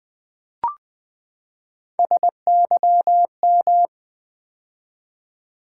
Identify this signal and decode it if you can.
Morse code, heard as a keyed tone, about 20 wpm, 700 Hz. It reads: SYM
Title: Build A Word - 3 Letters At A Time - 8-Letter Words - 20wpm